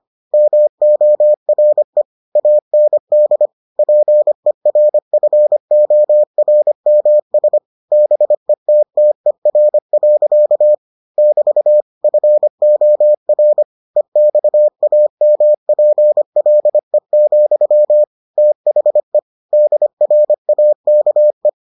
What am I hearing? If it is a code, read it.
Morse code, heard as a keyed tone, about 25 wpm, 600 Hz. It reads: MORE AND PERFORMS BETTER. <BT> FOR EXAMPLE, THE DRAKE